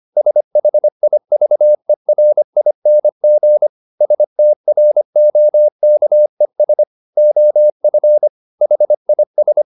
Morse code, heard as a keyed tone, about 25 wpm, 600 Hz. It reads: SHIVERING STROKES OF HIS
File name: the old man and the sea part 1 25wpm